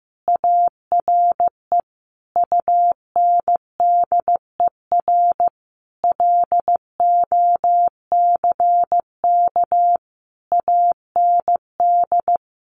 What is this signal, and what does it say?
Morse code, heard as a keyed tone, about 15 wpm, 700 Hz. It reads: ARE UNDER LOCK AND
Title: Sentences from Top 1300 Words 15wpm